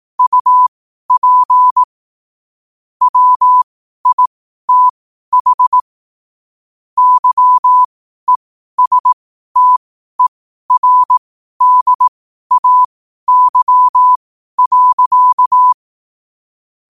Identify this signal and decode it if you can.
Morse code, heard as a keyed tone, about 18 wpm, spaced as for 12 wpm, 1000 Hz. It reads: UP WITH YESTERDAY.
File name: qotdfiles / 12wpm